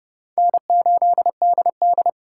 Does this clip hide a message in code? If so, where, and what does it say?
Yes: Morse code from 0.4 to 2.1 s, heard as a keyed tone, about 28 wpm, 700 Hz. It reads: N8DD